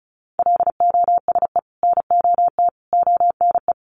Morse code, heard as a keyed tone, about 34 wpm, 700 Hz. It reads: LOSE NOT ONE
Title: Sentences from Top 1100 Words 35wpm